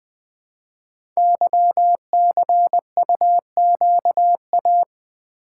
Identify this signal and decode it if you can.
Morse code, heard as a keyed tone, about 20 wpm, 700 Hz. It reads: YCUQA
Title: Code Groups - 5 Letters - 20wpm